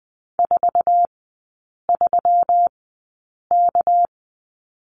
Morse code, heard as a keyed tone, about 20 wpm, spaced as for 8 wpm, 690 Hz. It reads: 43K
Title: 198 Lesson 32e - Letter k - Three Characters 20wpm